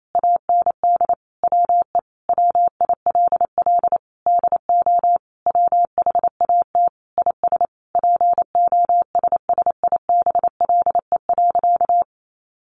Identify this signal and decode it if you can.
Morse code, heard as a keyed tone, about 28 wpm, 690 Hz. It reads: AND WE WILL DO WHAT IS POSSIBLE.
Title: Sentences from Top 900 Words 28wpm